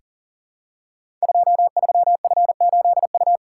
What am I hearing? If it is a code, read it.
Morse code, heard as a keyed tone, about 43 wpm, 700 Hz. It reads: 23F8U
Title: Code Groups - 5 Letter:Numbers - 40wpm